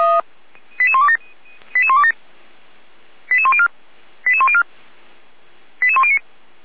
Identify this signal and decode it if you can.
Five-tone selective call (ZVEI), heard as quick runs of five tones, tones of about 70 ms. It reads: 88127 88104 88104 88109